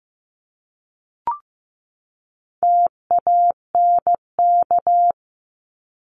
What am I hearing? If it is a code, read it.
Morse code, heard as a keyed tone, about 15 wpm, 700 Hz. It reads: TANK
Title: Top 1751-2000 Words 15wpm